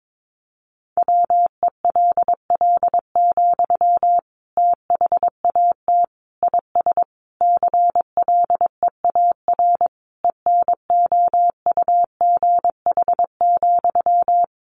Morse code, heard as a keyed tone, about 22 wpm, 700 Hz. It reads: WELL, THAT IS CLEAR ENOUGH,
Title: Sentences from Top 900 Words 22wpm